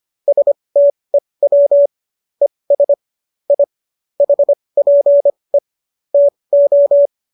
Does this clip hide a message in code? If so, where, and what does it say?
Yes: Morse code from 0.3 to 7.1 s, heard as a keyed tone, about 25 wpm, spaced as for 20 wpm, 570 Hz. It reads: STEW ES I HPE TO